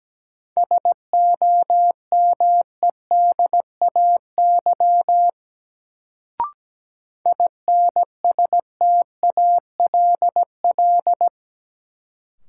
Morse code, heard as a keyed tone, about 17 wpm, 700 Hz. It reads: SOMEDAY INSTALL
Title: Sentences from Top 1600 Words - New Word Drill - 17wpm